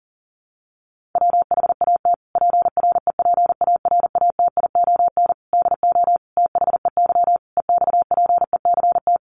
Morse code, heard as a keyed tone, about 40 wpm, 700 Hz. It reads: WHAT PREPARATION DO THEY EXPECT
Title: Sentences from Top 1400 Words 40wpm